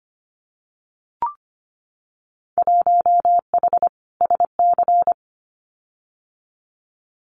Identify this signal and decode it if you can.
Morse code, heard as a keyed tone, about 25 wpm, 700 Hz. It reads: 1H SC
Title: Winter Field Day Exchanges 25wpm